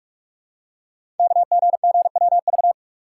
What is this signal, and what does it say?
Morse code, heard as a keyed tone, about 44 wpm, 700 Hz. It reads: KGGWV
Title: Code Groups - 5 Letters - 45wpm